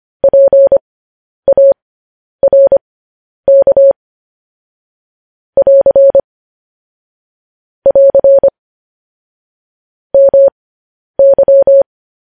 Morse code, heard as a keyed tone, about 25 wpm, spaced as for 10 wpm, 550 Hz. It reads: PARK <AR> <AR> MY
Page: news headlines in morse code at 10 wpm